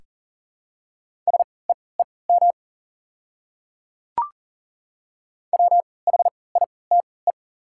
Morse code, heard as a keyed tone, about 40 wpm, spaced as for 23 wpm, 700 Hz. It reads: SEEM WHITE
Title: 3x Character Spacing - Top 300 Words - 40wpm